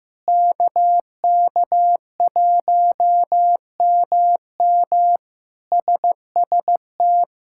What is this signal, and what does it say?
Morse code, heard as a keyed tone, about 15 wpm, 700 Hz. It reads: KK1MM SST
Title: K1USN SST QSO Practice - 15wpm